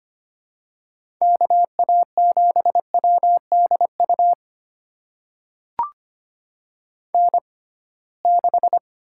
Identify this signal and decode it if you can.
Morse code, heard as a keyed tone, about 25 wpm, spaced as for 19 wpm, 700 Hz. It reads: KA7WDU N6
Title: USA Call Signs - 6x Character Spacing - 25wpm